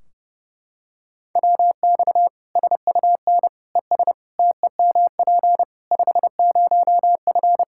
Morse code, heard as a keyed tone, about 30 wpm, 700 Hz. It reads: WX SUN ES TEMP 50F
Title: Practice QSO - Exchange 2 - Info - 30wpm